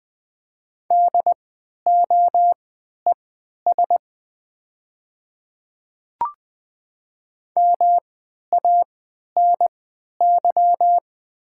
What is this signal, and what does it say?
Morse code, heard as a keyed tone, about 20 wpm, spaced as for 11 wpm, 700 Hz. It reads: DOES MANY
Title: 3x Character Spacing - Top 200 Words - 20wpm